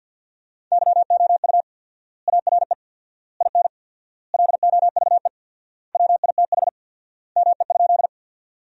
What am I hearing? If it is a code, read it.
Morse code, heard as a keyed tone, about 48 wpm, 700 Hz. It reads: YOU ARE IN LOVE WITH ME?
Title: 4x Word-Spacing - Sentences from Top 400 Words - 50wpm